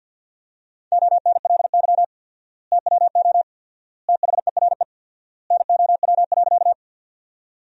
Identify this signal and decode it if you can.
Morse code, heard as a keyed tone, about 50 wpm, 700 Hz. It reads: ONLY TWO THERE NOW.
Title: 4x Word-Spacing - Sentences from Top 200 Words - 50wpm